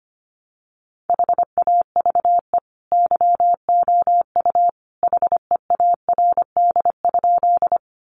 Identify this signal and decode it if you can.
Morse code, heard as a keyed tone, about 25 wpm, 700 Hz. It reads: HAVE YOU HEARD?
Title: Sentences from Top 400 Words 25wpm